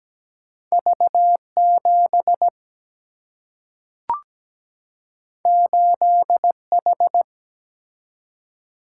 Morse code, heard as a keyed tone, about 17 wpm, 700 Hz. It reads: V7 8H
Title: Single-Letter Single-Number 17wpm